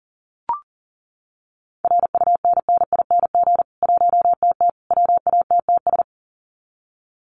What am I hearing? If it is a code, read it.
Morse code, heard as a keyed tone, about 40 wpm, 700 Hz. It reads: RUNNING 1TT WATTS
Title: Common QSO Phrases 40wpm